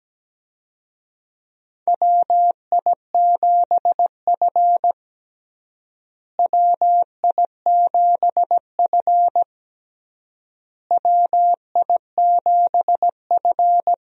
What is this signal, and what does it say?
Morse code, heard as a keyed tone, about 17 wpm, 700 Hz. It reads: WI7F WI7F WI7F
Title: USA Call Signs - 4 Characters - Repeated 3x - 17wpm